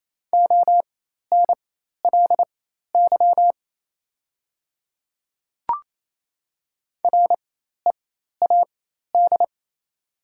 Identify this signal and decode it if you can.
Morse code, heard as a keyed tone, about 27 wpm, spaced as for 13 wpm, 700 Hz. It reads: ONLY READ
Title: 4x Character Spacing - Top 300 Words - 28wpm